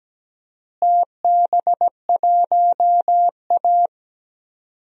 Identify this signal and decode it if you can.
Morse code, heard as a keyed tone, about 17 wpm, 700 Hz. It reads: TB1A
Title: DXCC-100 Call Signs - 4 Characters - Repeated 3x - 17wpm